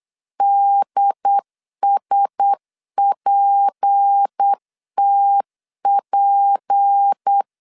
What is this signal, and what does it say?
Morse code, heard as a keyed tone, about 8 wpm, 800 Hz. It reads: DSPTP
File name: LEZIONI / 6 PKX0 / 6 P 8wpm 3025